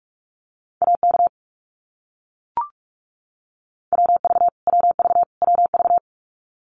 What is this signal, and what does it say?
Morse code, heard as a keyed tone, about 44 wpm, 700 Hz. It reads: AK WV WV WV
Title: US States Repeated 3x 45wpm